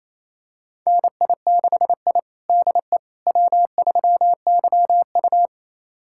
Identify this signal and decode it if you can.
Morse code, heard as a keyed tone, about 28 wpm, 700 Hz. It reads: NI6I DE W3YU
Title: USA Callsign DE Callsign - 4 Characters - 28wpm